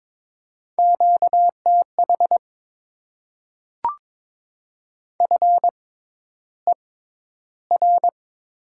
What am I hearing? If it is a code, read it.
Morse code, heard as a keyed tone, about 22 wpm, spaced as for 11 wpm, 690 Hz. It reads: QTH FER
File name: Top 20 QSO Elements - 6x Character Spacing - 22wpm